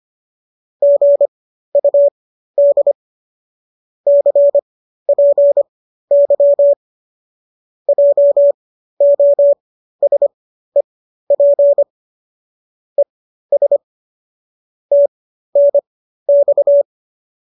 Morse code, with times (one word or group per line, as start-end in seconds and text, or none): 0.8-2.9 GUD
4.1-6.7 CPY
7.9-11.8 JOSEP
13.0-13.8 ES
14.9-16.8 TNX